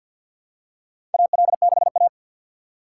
Morse code, heard as a keyed tone, about 52 wpm, 700 Hz. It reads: AL6A
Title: USA Call Signs - 4 Characters - 6x Character Spacing - 50wpm